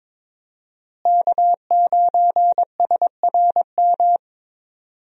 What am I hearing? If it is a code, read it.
Morse code, heard as a keyed tone, about 22 wpm, 700 Hz. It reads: K9SRM
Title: USA Call Signs - 5 Characters - Repeated 2x - 22wpm